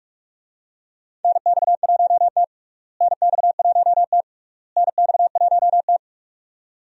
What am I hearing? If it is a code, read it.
Morse code, heard as a keyed tone, about 44 wpm, 700 Hz. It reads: NX1T NX1T NX1T
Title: USA Call Signs - 4 Characters - Repeated 3x - 45wpm